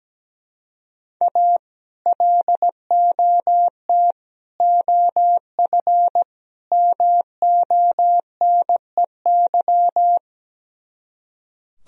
Morse code, heard as a keyed tone, about 17 wpm, 700 Hz. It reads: A LOT OF MONEY
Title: Speed-Racing - Sets of 4 Words 17wpm